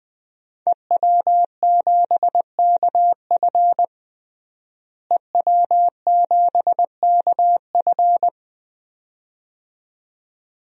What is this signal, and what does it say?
Morse code, heard as a keyed tone, about 20 wpm, 700 Hz. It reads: EW7KF EW7KF
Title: DXCC-100 Call Signs - 5 Characters - Repeated 2x - 20wpm